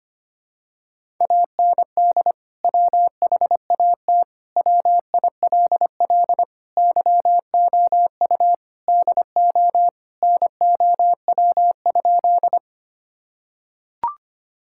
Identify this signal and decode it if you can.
Morse code, heard as a keyed tone, about 25 wpm, 700 Hz. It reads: AND WHAT WILL YOU DO NOW?
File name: practice / 25wpm / Sentences with 2-Word Phrases - 25wpm